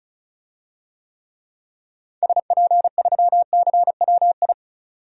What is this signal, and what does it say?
Morse code, heard as a keyed tone, about 34 wpm, 700 Hz. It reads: SP3CWI
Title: DXCC-100 Call Signs - 6 Characters - 35wpm